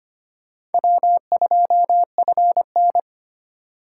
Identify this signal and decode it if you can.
Morse code, heard as a keyed tone, about 30 wpm, 700 Hz. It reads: W2FN
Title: Speed Racing - USA Call Signs 25wpm